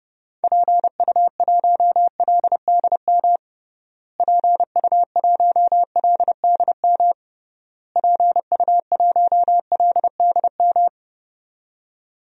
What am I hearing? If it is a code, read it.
Morse code, heard as a keyed tone, about 32 wpm, 700 Hz. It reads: PU1LDM PU1LDM PU1LDM
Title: DXCC-100 Call Signs - 6 Characters - Repeated 3x - 30wpm